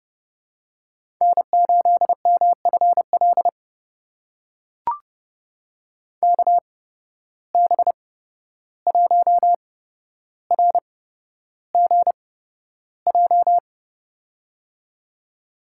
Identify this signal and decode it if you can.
Morse code, heard as a keyed tone, about 30 wpm, spaced as for 12 wpm, 700 Hz. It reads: N8MFL KB1RGJ
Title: USA Call Signs - 8x Character Spacing - 30wpm